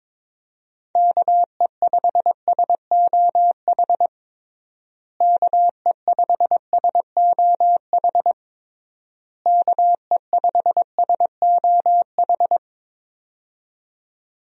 Morse code, heard as a keyed tone, about 22 wpm, 700 Hz. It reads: KE5SOH KE5SOH KE5SOH
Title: USA Call Signs - 6 Characters - Repeated 3x - 22wpm